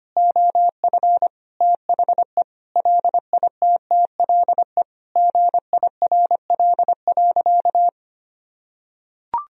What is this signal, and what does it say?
Morse code, heard as a keyed tone, about 25 wpm, 700 Hz. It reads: OF THE LITTLE GIRL.
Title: Sentences from Top 400 Words and Abbreviations 25wpm